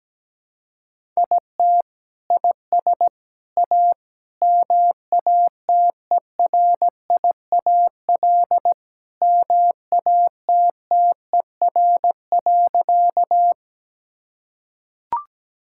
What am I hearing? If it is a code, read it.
Morse code, heard as a keyed tone, about 17 wpm, 700 Hz. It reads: IT IS A MATERIAL MATTER.
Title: Sentences from Top 1000 Words - Review Part 5 - 17wpm